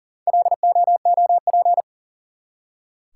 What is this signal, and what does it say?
Morse code, heard as a keyed tone, about 43 wpm, 700 Hz. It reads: LOOP
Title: Top 180 QSO Elements 40wpm